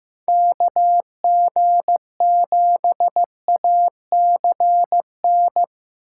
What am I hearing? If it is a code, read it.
Morse code, heard as a keyed tone, about 15 wpm, 700 Hz. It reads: KG7ACN
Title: USA Call Signs - 6 Characters - 2x Character Spacing - 15wpm